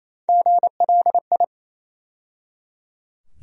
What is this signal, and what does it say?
Morse code, heard as a keyed tone, about 28 wpm, 700 Hz. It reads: GLI
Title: USA Call Sign Postfixes - 3 Letters 28wpm